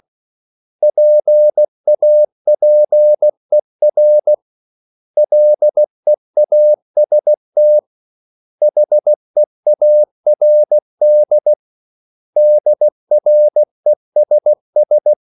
Morse code, heard as a keyed tone, about 16 wpm, 600 Hz. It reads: PAPER LEAST HEARD DRESS
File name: CW / Practice / 500MostCommonEnglishWords16